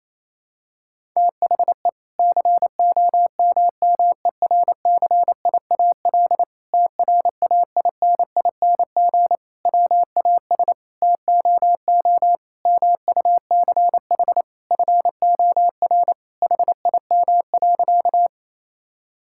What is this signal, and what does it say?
Morse code, heard as a keyed tone, about 28 wpm, 700 Hz. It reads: THE COMMERCIAL TRAINING WAS TOO MUCH FOR HIM.
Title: Sentences from Top 1000 Words - Review Part 5 - 28wpm